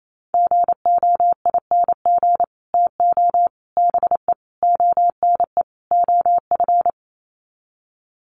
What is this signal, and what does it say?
Morse code, heard as a keyed tone, about 28 wpm, 700 Hz. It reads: GOING TO BE ONE OF
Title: Sets of 5 Words Encore2 28wpm